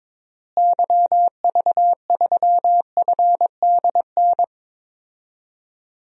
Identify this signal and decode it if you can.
Morse code, heard as a keyed tone, about 22 wpm, 700 Hz. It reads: YV3FDN
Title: DXCC-100 Call Signs - 6 Characters - Repeated 3x - 22wpm